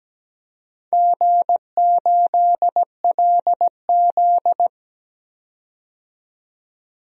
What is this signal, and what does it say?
Morse code, heard as a keyed tone, about 17 wpm, 700 Hz. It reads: G8LZ
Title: DXCC-100 Call Signs - 4 Characters - 17wpm